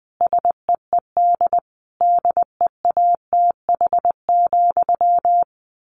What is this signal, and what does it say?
Morse code, heard as a keyed tone, about 20 wpm, 700 Hz. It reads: SEED DEATH,
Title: A Sound Of Thunder 20wpm